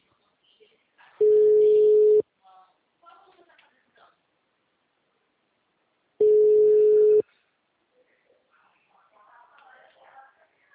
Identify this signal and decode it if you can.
Ringback tone (Europe, 425 Hz), with speech.